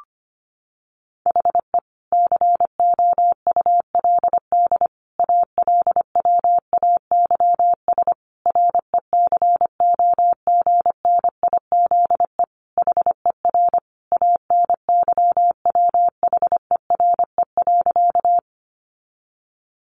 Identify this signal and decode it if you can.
Morse code, heard as a keyed tone, about 25 wpm, 700 Hz. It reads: HE COULD ALWAYS RECOGNIZE HER ANYWHERE.
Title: Sentences from Top 1400 Words 25wpm